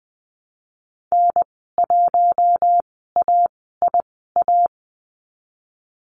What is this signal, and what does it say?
Morse code, heard as a keyed tone, about 20 wpm, spaced as for 15 wpm, 700 Hz. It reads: N1AIA
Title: 042 Lesson 8g - Number 1 - Call Signs 20wpm